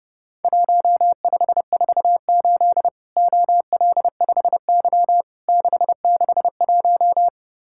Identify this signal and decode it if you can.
Morse code, heard as a keyed tone, about 30 wpm, 700 Hz. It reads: 1548 OL5Y 661
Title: Worked All Europe DX Contest - QTC Exchange - 30wpm